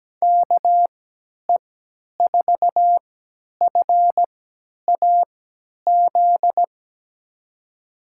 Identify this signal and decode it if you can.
Morse code, heard as a keyed tone, about 17 wpm, spaced as for 10 wpm, 700 Hz. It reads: KE4FAZ
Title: USA Call Signs - 6 Characters - 3x Character Spacing - 17wpm